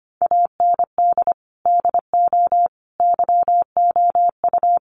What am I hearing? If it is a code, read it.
Morse code, heard as a keyed tone, about 25 wpm, 700 Hz. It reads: AND DO YOU
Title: Sentences from Top 100 Words 25wpm